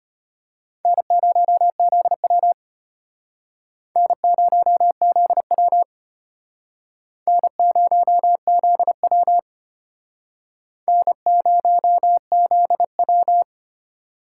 Morse code, heard as a keyed tone, about 32 wpm, 700 Hz. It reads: N0ZW N0ZW N0ZW N0ZW